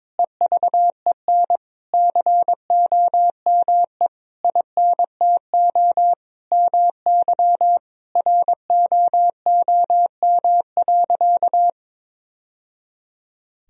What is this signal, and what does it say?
Morse code, heard as a keyed tone, about 22 wpm, 700 Hz. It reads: EVEN COME INTO MY ROOM.
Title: Sentences from Top 1000 Words - Review Part 1 - 22wpm